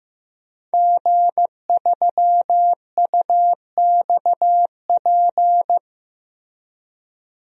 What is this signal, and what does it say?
Morse code, heard as a keyed tone, about 15 wpm, 700 Hz. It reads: G3UXP
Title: DXCC-100 Call Signs - 5 Characters - Repeated 2x - 15wpm